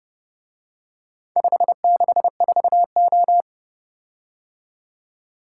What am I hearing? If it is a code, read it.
Morse code, heard as a keyed tone, about 29 wpm, 700 Hz. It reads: 564O